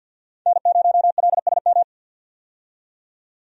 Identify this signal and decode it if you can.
Morse code, heard as a keyed tone, about 50 wpm, 700 Hz. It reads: N0LSM
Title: USA Call Signs - 5 Characters - 50wpm